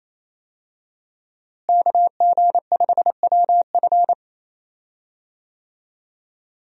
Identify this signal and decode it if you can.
Morse code, heard as a keyed tone, about 27 wpm, 700 Hz. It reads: KG5WF
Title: USA Call Signs - 5 Characters - Repeated 3x - 28wpm